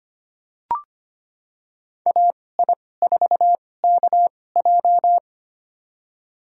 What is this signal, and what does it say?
Morse code, heard as a keyed tone, about 25 wpm, spaced as for 18 wpm, 700 Hz. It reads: AI4KJ
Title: USA Call Signs - 2x Character Spacing - 25wpm